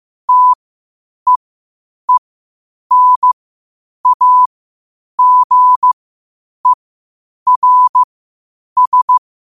Morse code, heard as a keyed tone, about 15 wpm, spaced as for 8 wpm, 1000 Hz. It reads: TEENAGERS
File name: qotdfiles / 8wpm